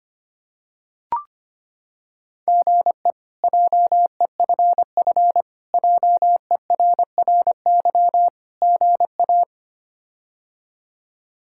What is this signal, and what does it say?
Morse code, heard as a keyed tone, about 25 wpm, 700 Hz. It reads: GE JEFF JERRY GA
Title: K1USN SST QSO Practice - 25wpm